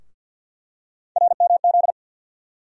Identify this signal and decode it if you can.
Morse code, heard as a keyed tone, about 48 wpm, 700 Hz. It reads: RMZ
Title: USA Call Sign Postfixes - 3 Letters 50wpm